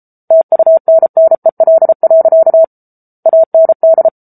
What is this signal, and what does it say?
Morse code, heard as a keyed tone, about 34 wpm, 650 Hz. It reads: TUNNEL. AND
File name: Book Two / 07-Chapter 7 - 33 WPM